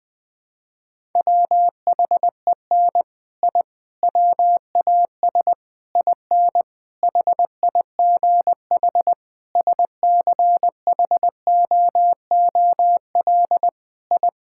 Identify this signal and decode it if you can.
Morse code, heard as a keyed tone, about 20 wpm, 700 Hz. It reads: WHEN I WAS IN HIGH SCHOOL I
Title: Sentences with 6-Word Phrases - Spoken Beginning - 20wpm